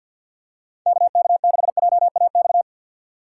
Morse code, heard as a keyed tone, about 50 wpm, 700 Hz. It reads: KK6JAX
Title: USA Call Signs - 6 Characters - 4x Character Spacing - 50wpm